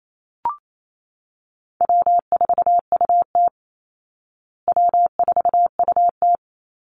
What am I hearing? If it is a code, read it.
Morse code, heard as a keyed tone, about 28 wpm, 700 Hz. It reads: W4UT W4UT
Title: Easy Call Signs Repeated 2x - 28wpm